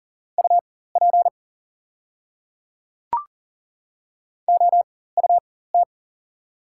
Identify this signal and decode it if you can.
Morse code, heard as a keyed tone, about 40 wpm, spaced as for 19 wpm, 700 Hz. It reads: UP OUT
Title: 4x Character Spacing - Top 200 Words - 40wpm